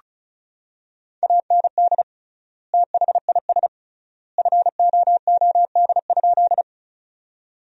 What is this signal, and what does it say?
Morse code, heard as a keyed tone, about 36 wpm, 700 Hz. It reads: AND THIS FOOD?
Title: 3x Word-Spacing - Sentences from Top 300 Words - 35wpm